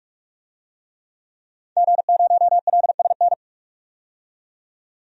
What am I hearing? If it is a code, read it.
Morse code, heard as a keyed tone, about 46 wpm, 700 Hz. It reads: G0LSN
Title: DXCC-100 Call Signs - 5 Characters - 45wpm